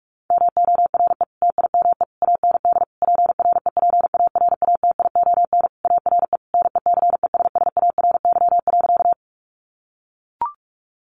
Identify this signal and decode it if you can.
Morse code, heard as a keyed tone, about 44 wpm, 700 Hz. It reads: MORE TIME AND PREPARATION ARE NECESSARY.
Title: Sentences from Top 1400 Words 45wpm